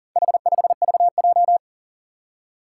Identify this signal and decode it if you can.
Morse code, heard as a keyed tone, about 38 wpm, 700 Hz. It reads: H5VJ
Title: Four Challenging Letter-Number Combinations 40wpm